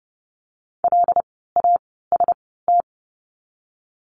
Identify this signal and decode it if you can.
Morse code, heard as a keyed tone, about 30 wpm, spaced as for 17 wpm, 700 Hz. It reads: LAST
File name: Lesson 16f - Letter c - Words 30wpm